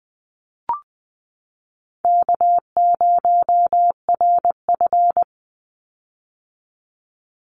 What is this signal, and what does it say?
Morse code, heard as a keyed tone, about 20 wpm, 700 Hz. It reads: K0RF
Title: Call Signs - 20wpm